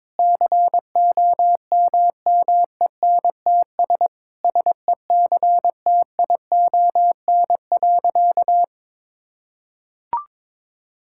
Morse code, heard as a keyed tone, about 22 wpm, 700 Hz. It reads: COMMENTS SECTION.
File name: Sentences with 9-Word Phrases - 22wpm